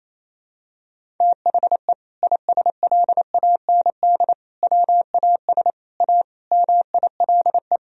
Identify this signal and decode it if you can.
Morse code, heard as a keyed tone, about 28 wpm, 700 Hz. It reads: THE ISLAND WAS A MILE
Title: Sentences from Top 500 Words and Abbreviations 28wpm